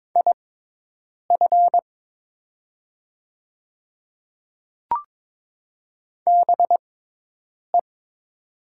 Morse code, heard as a keyed tone, about 22 wpm, spaced as for 8 wpm, 690 Hz. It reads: IF BE
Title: 6x Character Spacing - Top 200 Words - 22wpm